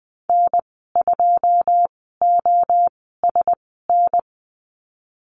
Morse code, heard as a keyed tone, about 20 wpm, spaced as for 14 wpm, 700 Hz. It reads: N2OSN
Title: 088 Lesson 15g - Number 5 - Call Signs 20wpm